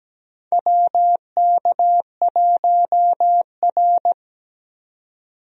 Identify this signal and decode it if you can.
Morse code, heard as a keyed tone, about 17 wpm, 700 Hz. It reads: WK1R
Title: USA Call Signs - 4 Characters - 17wpm